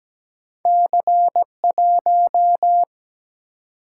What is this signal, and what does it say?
Morse code, heard as a keyed tone, about 17 wpm, 700 Hz. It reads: C 1
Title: Single-Letter Single-Number 17wpm